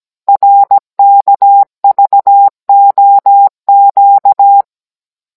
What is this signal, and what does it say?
Morse code, heard as a keyed tone, about 17 wpm, 800 Hz. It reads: RKVOQ